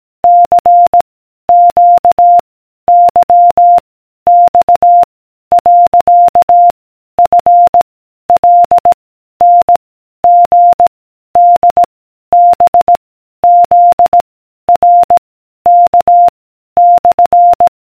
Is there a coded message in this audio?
Yes: Morse code from 0.2 to 17.7 s, heard as a keyed tone, about 18 wpm, 700 Hz. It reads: C Q Y X . F L N G D B Z R K /